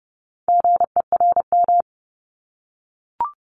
Morse code, heard as a keyed tone, about 30 wpm, 700 Hz. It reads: GERM
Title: Multi-letter Phonemes 30wpm